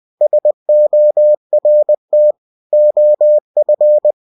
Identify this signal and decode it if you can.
Morse code, heard as a keyed tone, about 20 wpm, 600 Hz. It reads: SORT OF
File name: cw-books / winnie-the-pooh / winnie-20wpm-page-0014